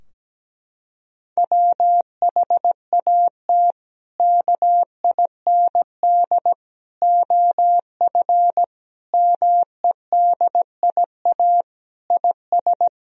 Morse code, heard as a keyed tone, about 17 wpm, 700 Hz. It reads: WHAT KIND OF MEDIA IS
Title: Sentences from Top 1000 Words - Review Part 4 - 17wpm